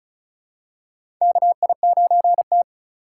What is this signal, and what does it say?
Morse code, heard as a keyed tone, about 36 wpm, 700 Hz. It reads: KI9T
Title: USA Call Signs - 4 Characters - 4x Character Spacing - 35wpm